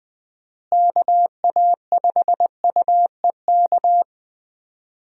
Morse code, heard as a keyed tone, about 20 wpm, 700 Hz. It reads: KA5UEK
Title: USA Call Signs - 6 Characters - Repeated 3x - 20wpm